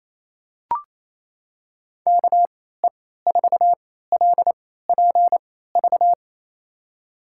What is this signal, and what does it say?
Morse code, heard as a keyed tone, about 28 wpm, spaced as for 16 wpm, 700 Hz. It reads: KE4LPV
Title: USA Call Signs - 3x Character Spacing - 28wpm